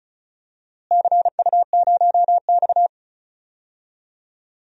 Morse code, heard as a keyed tone, about 34 wpm, 700 Hz. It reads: CU0X
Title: DXCC-100 Call Signs - 4 Characters - Repeated 2x - 35wpm